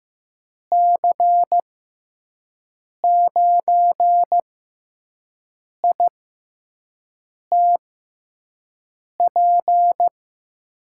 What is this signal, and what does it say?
Morse code, heard as a keyed tone, about 15 wpm, spaced as for 5 wpm, 700 Hz. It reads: C9ITP
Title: USA Call Signs - 6x Character Spacing - 15wpm